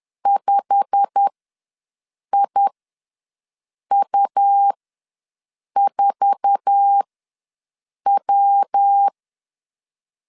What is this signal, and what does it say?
Morse code, heard as a keyed tone, about 10 wpm, spaced as for 6 wpm, 800 Hz. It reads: 5IU4W
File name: LEZIONI / 10 SIGNS / 10B INT ws 10wpm 4134